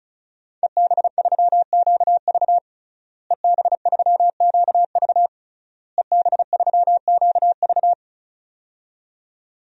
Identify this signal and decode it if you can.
Morse code, heard as a keyed tone, about 34 wpm, 700 Hz. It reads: EB3QV EB3QV EB3QV
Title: DXCC-100 Call Signs - 5 Characters - Repeated 3x - 35wpm